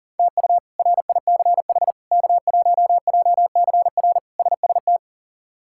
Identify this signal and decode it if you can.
Morse code, heard as a keyed tone, about 40 wpm, 700 Hz. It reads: TU RICH K1JCR SST
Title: K1USN SST QSO Practice - 40wpm